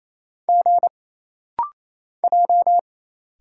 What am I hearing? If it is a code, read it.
Morse code, heard as a keyed tone, about 28 wpm, 700 Hz. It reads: G J
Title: Single Letters - Rapid-Fire 28wpm